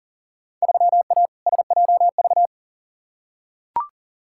Morse code, heard as a keyed tone, about 40 wpm, 700 Hz. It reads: 3A SJV
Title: Field Day Exchanges Encore - 40wpm